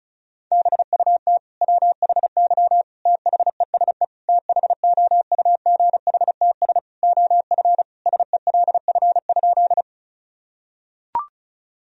Morse code, heard as a keyed tone, about 36 wpm, 700 Hz. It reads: BUT WHY THESE THOUGHTS OF SELF?
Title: Sentences from Top 1000 Words - Review Part 1 - 35wpm